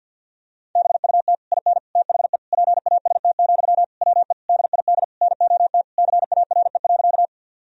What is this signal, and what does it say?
Morse code, heard as a keyed tone, about 50 wpm, 700 Hz. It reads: BUT IN THE PAST, WE DID NOT CARE.
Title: Sentences from Top 1000 Words - Review Part 3 - 50wpm